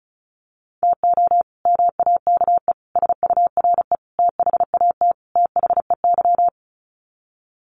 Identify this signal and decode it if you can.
Morse code, heard as a keyed tone, about 34 wpm, 700 Hz. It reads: TO MAKE SURE THAT THEY
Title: Sets of 5 Words Encore2 35wpm